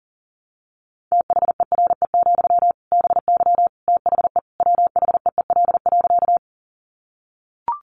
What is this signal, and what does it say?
Morse code, heard as a keyed tone, about 40 wpm, 700 Hz. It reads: THERE, BY THE WHEEL.
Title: Sentences from Top 500 Words 40wpm